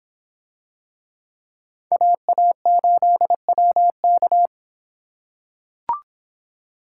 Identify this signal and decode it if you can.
Morse code, heard as a keyed tone, about 26 wpm, 700 Hz. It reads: AA8WK